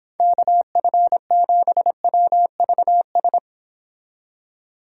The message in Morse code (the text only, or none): KF7WVS